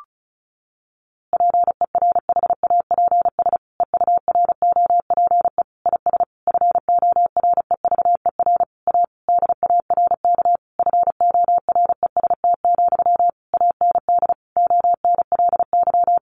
Morse code, heard as a keyed tone, about 34 wpm, 700 Hz. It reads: PERHAPS EUROPE IS FOREVER A DARK FOREST, AND ONLY
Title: A Sound Of Thunder 35wpm